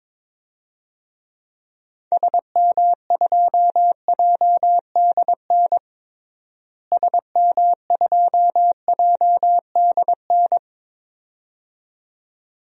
Morse code, heard as a keyed tone, about 21 wpm, 700 Hz. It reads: SM2JDN SM2JDN
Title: DXCC-100 Call Signs - 6 Characters - Repeated 2x - 22wpm